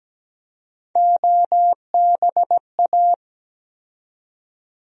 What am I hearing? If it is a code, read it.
Morse code, heard as a keyed tone, about 17 wpm, 700 Hz. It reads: OBA
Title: USA Call Sign Postfixes - 3 Letters 17wpm